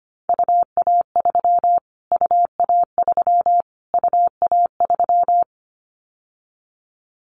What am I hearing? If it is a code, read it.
Morse code, heard as a keyed tone, about 25 wpm, 700 Hz. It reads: UA3 UA3 UA3
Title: Callsign Prefixes for Top 25 DXCC - Repeated 3x - 25wpm